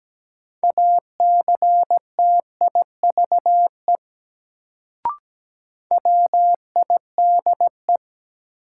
Morse code, heard as a keyed tone, about 17 wpm, 700 Hz. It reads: ACTIVE WIDE
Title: Sentences from Top 900 Words - New Word Drill - 17wpm